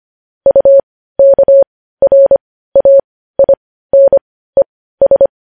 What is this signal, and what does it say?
Morse code, heard as a keyed tone, about 25 wpm, spaced as for 15 wpm, 550 Hz. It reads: UKRAINES
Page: News Headlines in Morse Code at 15 WPM